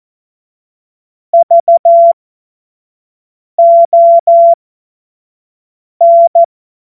Morse code, heard as a keyed tone, about 14 wpm, spaced as for 5 wpm, 680 Hz. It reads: VON
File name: Morsekurs / Witze / Witz 07.November 2023 25 F70